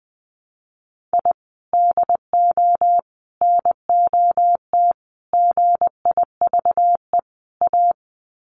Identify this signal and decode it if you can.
Morse code, heard as a keyed tone, about 20 wpm, 700 Hz. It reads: I DO NOT GIVE A
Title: Sets of 5 Words Encore2 20wpm